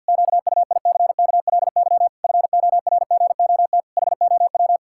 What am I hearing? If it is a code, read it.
Morse code, heard as a keyed tone, about 50 wpm, 700 Hz. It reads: QUICKLY FORGOT HOW
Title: Pangram Challenge v2 - 50wpm